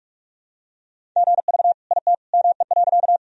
Morse code, heard as a keyed tone, about 45 wpm, 700 Hz. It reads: GV IT ME.
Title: Sentences from Top 200 Words and Abbreviations 45wpm